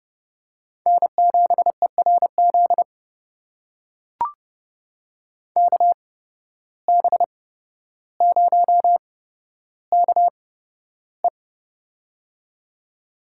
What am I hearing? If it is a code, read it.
Morse code, heard as a keyed tone, about 30 wpm, spaced as for 13 wpm, 700 Hz. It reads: N7ERZ KB0KE
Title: USA Call Signs - 5 Characters - 8x Character Spacing - 30wpm